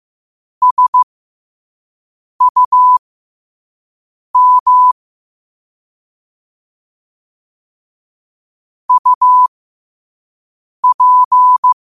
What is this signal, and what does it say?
Morse code, heard as a keyed tone, about 15 wpm, spaced as for 5 wpm, 1000 Hz. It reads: SUM UP
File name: qotdfiles / 5wpm